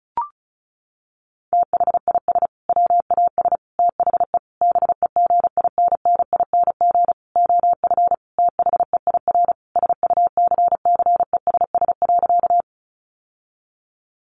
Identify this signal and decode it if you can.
Morse code, heard as a keyed tone, about 34 wpm, 700 Hz. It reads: THIS WAS THE BEGINNING OF THEIR SUCCESS.
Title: Sentences from Top 900 Words 35wpm